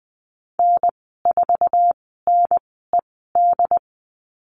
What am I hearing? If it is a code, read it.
Morse code, heard as a keyed tone, about 20 wpm, spaced as for 14 wpm, 700 Hz. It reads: N4NED
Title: 088 Lesson 15g - Number 5 - Call Signs 20wpm